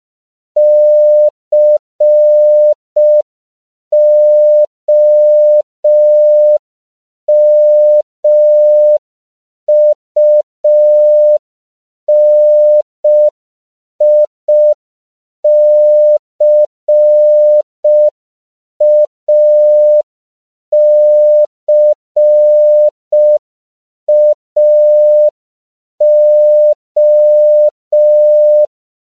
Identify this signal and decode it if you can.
Morse code, heard as a keyed tone, about 5 wpm, 600 Hz. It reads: COMUNICACAO
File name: CW / 5PPM